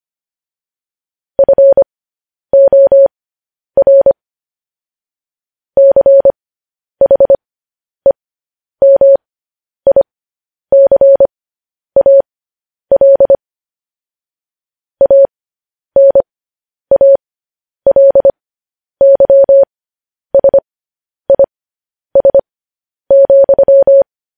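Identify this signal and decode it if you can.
Morse code, heard as a keyed tone, about 25 wpm, spaced as for 10 wpm, 550 Hz. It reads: FOR CHEMICAL ANALYSIS,
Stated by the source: Morse code news headlines sent at 10 words per minute.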